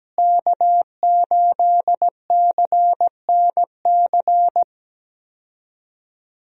Morse code, heard as a keyed tone, about 17 wpm, 700 Hz. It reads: K8CNC
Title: USA Call Signs - 5 Characters - Repeated 3x - 17wpm